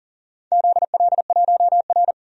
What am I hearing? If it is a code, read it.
Morse code, heard as a keyed tone, about 40 wpm, 700 Hz. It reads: ZL1R
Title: DXCC-100 Call Signs - 4 Characters - 40wpm